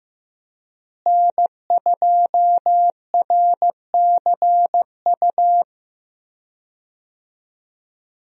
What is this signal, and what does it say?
Morse code, heard as a keyed tone, about 15 wpm, 700 Hz. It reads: N2RCU
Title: USA Call Signs - 5 Characters - Repeated 2x - 15wpm